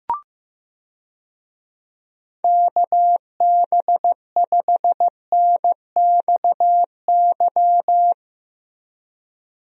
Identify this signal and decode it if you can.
Morse code, heard as a keyed tone, about 15 wpm, 700 Hz. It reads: KB5NXY
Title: USA Call Signs - 6 Characters - Repeated 3x - 15wpm